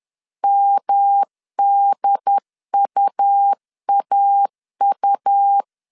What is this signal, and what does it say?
Morse code, heard as a keyed tone, about 10 wpm, 800 Hz. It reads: MDUAU
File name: CW / DET / 1 3 1500C S C 10wpm 3335